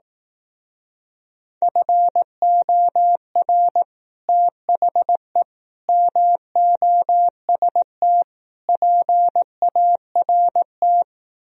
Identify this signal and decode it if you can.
Morse code, heard as a keyed tone, about 18 wpm, 700 Hz. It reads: FOR THE MOST PART
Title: Speed-Racing - Sets of 4 Words 15wpm